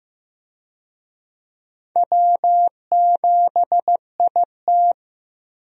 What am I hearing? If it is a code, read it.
Morse code, heard as a keyed tone, about 16 wpm, 700 Hz. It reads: W7IT
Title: USA Call Signs - 4 Characters - Repeated 3x - 15wpm